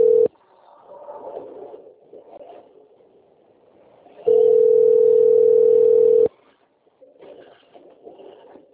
Ringback tone (North America), 440 + 480 Hz.
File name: out- - - - - .37793